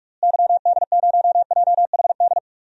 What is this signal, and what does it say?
Morse code, heard as a keyed tone, about 45 wpm, 700 Hz. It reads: YD0JHD
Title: DXCC-100 Call Signs - 6 Characters - Repeated 2x - 45wpm